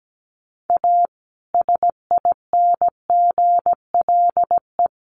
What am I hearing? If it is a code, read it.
Morse code, heard as a keyed tone, about 17 wpm, 700 Hz. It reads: A SINGLE
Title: Sentences from Top 1100 Words 17wpm